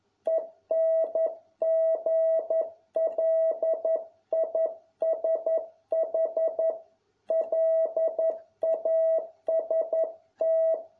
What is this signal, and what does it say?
Morse code, heard as a keyed tone, about 11 wpm, 650 Hz. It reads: ENGLISH LAST